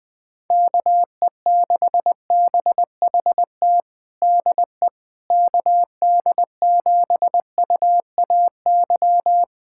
Morse code, heard as a keyed tone, about 20 wpm, 700 Hz. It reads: KE6BHT DE KD7UAY
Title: USA Callsign DE Callsign 20wpm